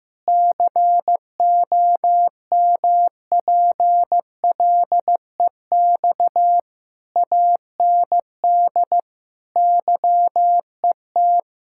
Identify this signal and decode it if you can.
Morse code, heard as a keyed tone, about 15 wpm, 700 Hz. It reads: COMPLEX AND YET
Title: Sentences from Top 1000 Words - Review Part 6 - 15wpm